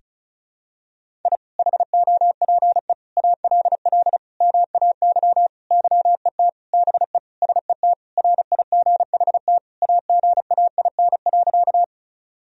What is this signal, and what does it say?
Morse code, heard as a keyed tone, about 34 wpm, 690 Hz. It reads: I HOPE ALL MAY YET BE SET RIGHT AGAIN.
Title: Sentences from Top 1000 Words - Review Part 1 - 35wpm